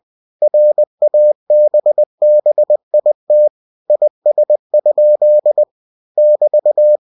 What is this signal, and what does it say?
Morse code, heard as a keyed tone, about 20 wpm, 600 Hz. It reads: RABBIT IS? <BT>
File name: cw-books / winnie-the-pooh / winnie-20wpm-page-0014